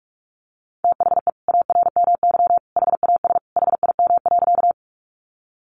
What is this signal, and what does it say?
Morse code, heard as a keyed tone, about 44 wpm, 700 Hz. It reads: THE ARMY HAS HIM.
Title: Sentences from Top 800 Words 45wpm